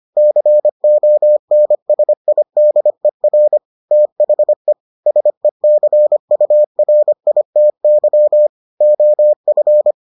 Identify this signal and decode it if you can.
Morse code, heard as a keyed tone, about 25 wpm, 600 Hz. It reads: CONSIDER THE SECURITY OF